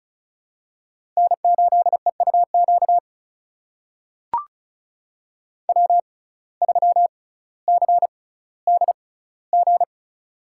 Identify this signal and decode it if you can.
Morse code, heard as a keyed tone, about 36 wpm, spaced as for 18 wpm, 700 Hz. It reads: N8EUQ W3CDG
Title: USA Call Signs - 5 Characters - 6x Character Spacing - 35wpm